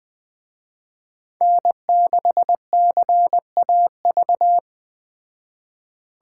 Morse code, heard as a keyed tone, about 20 wpm, 700 Hz. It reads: N6CAV
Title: Speed Racing - USA 1x3 Call Signs 17wpm